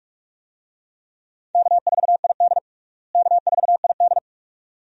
Morse code, heard as a keyed tone, about 46 wpm, 700 Hz. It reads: K4ID K4ID
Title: USA Call Signs - 4 Characters - Repeated 2x - 45wpm